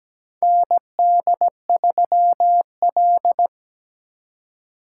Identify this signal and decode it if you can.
Morse code, heard as a keyed tone, about 17 wpm, 700 Hz. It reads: ND3L
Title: K1USN SST QSO Practice - 17wpm